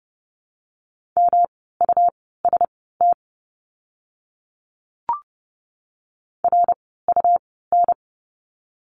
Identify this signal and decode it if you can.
Morse code, heard as a keyed tone, about 30 wpm, spaced as for 17 wpm, 700 Hz. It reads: MUST RUN
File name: course / 30:10wpm / 237 Lesson 38f - Letter q - Words 30wpm